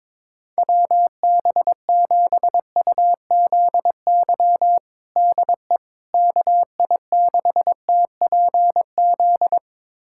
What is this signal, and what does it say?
Morse code, heard as a keyed tone, about 22 wpm, 700 Hz. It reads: WB7UZY DE KI6TPZ
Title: USA Callsign DE Callsign - 6 Characters - 22wpm